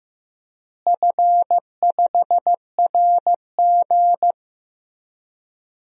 Morse code, heard as a keyed tone, about 15 wpm, 700 Hz. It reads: F5RG
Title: DXCC-100 Call Signs - 4 Characters - Repeated 3x - 15wpm